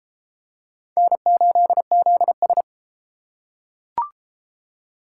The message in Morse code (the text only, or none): N8ZS